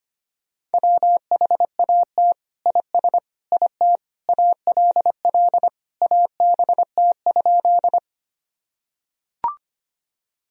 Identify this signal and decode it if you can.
Morse code, heard as a keyed tone, about 25 wpm, 700 Hz. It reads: WHAT IS IT ALL ABT?
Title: Sentences from Top 100 Words and Abbreviations 25wpm